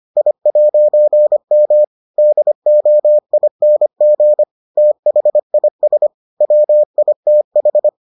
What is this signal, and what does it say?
Morse code, heard as a keyed tone, about 25 wpm, 600 Hz. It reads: I'M DOING THIS WITH